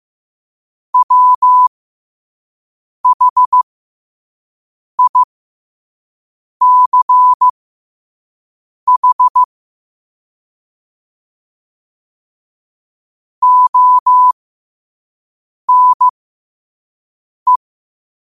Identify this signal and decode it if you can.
Morse code, heard as a keyed tone, about 15 wpm, spaced as for 5 wpm, 1000 Hz. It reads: WHICH ONE
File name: qotdfiles / 5wpm